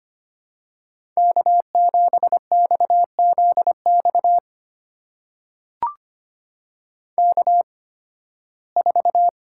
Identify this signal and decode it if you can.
Morse code, heard as a keyed tone, about 25 wpm, spaced as for 16 wpm, 700 Hz. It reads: K7XZX K4
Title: USA Call Signs - 5 Characters - 8x Character Spacing - 25wpm